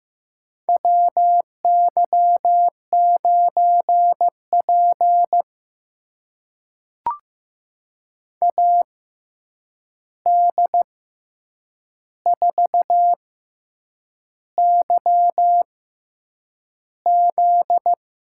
Morse code, heard as a keyed tone, about 15 wpm, spaced as for 7 wpm, 700 Hz. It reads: WY9P AD4YZ
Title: USA Call Signs - 6x Character Spacing - 15wpm